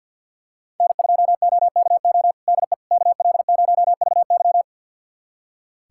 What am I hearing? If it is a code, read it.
Morse code, heard as a keyed tone, about 47 wpm, 700 Hz. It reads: N2OKO DE KL0VY
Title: USA Callsign DE Callsign 50wpm